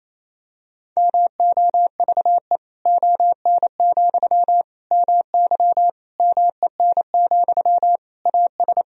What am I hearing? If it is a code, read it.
Morse code, heard as a keyed tone, about 28 wpm, 700 Hz. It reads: MOVE ON, MY MEN, AS
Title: Sentences from Top 500 Words and Abbreviations 28wpm